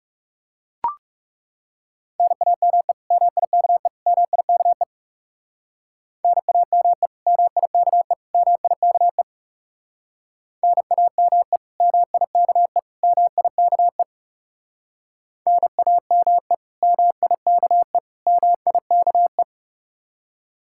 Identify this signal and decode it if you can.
Morse code, heard as a keyed tone, about 37 wpm, 700 Hz. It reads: NAME MIKE MIKE NAME MIKE MIKE NAME MIKE MIKE NAME MIKE MIKE
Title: Speed-Racing Common QSO Phrases - 30wpm